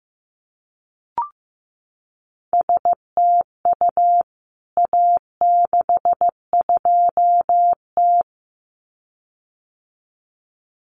Morse code, heard as a keyed tone, about 15 wpm, 700 Hz. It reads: STU A62T
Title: CWT Exchange 15wpm